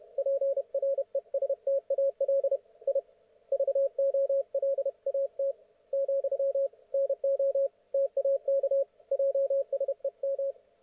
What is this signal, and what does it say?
Morse code, heard as a keyed tone, about 31 wpm, 550 Hz. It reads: PRESTAL I VOLAT , NO TAK JSEM